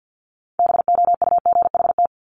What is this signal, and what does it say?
Morse code, heard as a keyed tone, about 50 wpm, 690 Hz. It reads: BOUGHT
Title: Multi-letter Phonemes Repeated 3x 50wpm